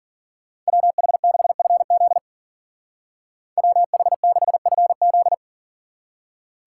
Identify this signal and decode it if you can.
Morse code, heard as a keyed tone, about 51 wpm, 700 Hz. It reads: WH6FZ WH6FZ